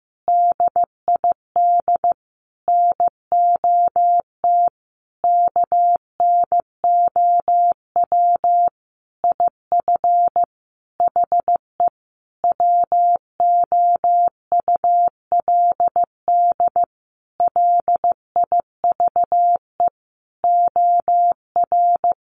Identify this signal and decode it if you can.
Morse code, heard as a keyed tone, about 15 wpm, 700 Hz. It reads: DID NOT KNOW IF HE WOULD LIVE OR
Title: Sentences from Top 1300 Words 15wpm